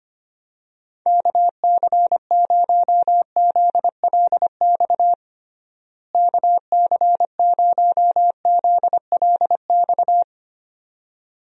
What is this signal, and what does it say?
Morse code, heard as a keyed tone, about 25 wpm, 700 Hz. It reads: KC0ZLX KC0ZLX
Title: USA Call Signs - 6 Characters - Repeated 2x - 25wpm